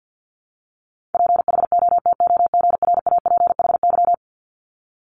Morse code, heard as a keyed tone, about 50 wpm, 690 Hz. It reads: PHOTOGRAPHY
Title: Multi-letter Phonemes 50wpm